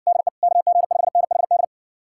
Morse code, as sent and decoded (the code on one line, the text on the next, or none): -.. / -.- --. ..... -. .... -..
DE KG5NHD